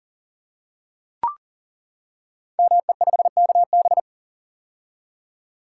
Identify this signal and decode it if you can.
Morse code, heard as a keyed tone, about 43 wpm, 700 Hz. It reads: ME5KB
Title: Code Groups - 5 Letter:Numbers - 40wpm